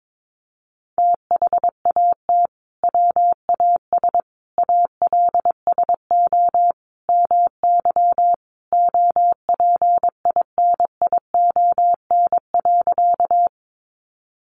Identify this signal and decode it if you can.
Morse code, heard as a keyed tone, about 22 wpm, 700 Hz. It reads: THAT WAS ALSO MY OPINION.
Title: Sentences from Top 1200 Words 22wpm